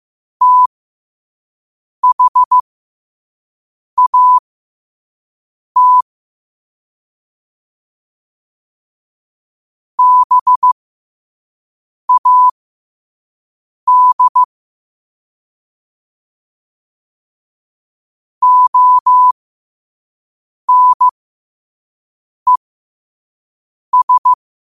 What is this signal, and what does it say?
Morse code, heard as a keyed tone, about 15 wpm, spaced as for 5 wpm, 1000 Hz. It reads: THAT BAD ONES
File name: qotdfiles / 5wpm